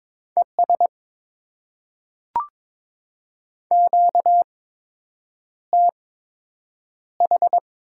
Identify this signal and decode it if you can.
Morse code, heard as a keyed tone, about 22 wpm, spaced as for 8 wpm, 700 Hz. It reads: ES QTH
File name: Top 20 QSO Elements - 8x Character Spacing - 22wpm